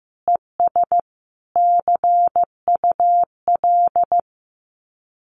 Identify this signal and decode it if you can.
Morse code, heard as a keyed tone, about 15 wpm, 700 Hz. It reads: ES CUL
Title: Common QSO Phrases Encore 15wpm